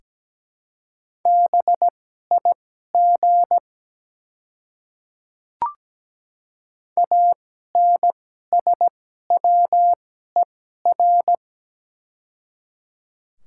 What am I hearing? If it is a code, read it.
Morse code, heard as a keyed tone, about 17 wpm, spaced as for 12 wpm, 700 Hz. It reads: BIG ANSWER
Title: 2x Character Spacing - Top 300 Words - 17wpm